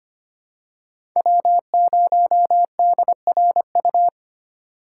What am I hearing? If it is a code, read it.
Morse code, heard as a keyed tone, about 25 wpm, 700 Hz. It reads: W0DRU
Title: USA Call Signs - 5 Characters - 2x Character Spacing - 25wpm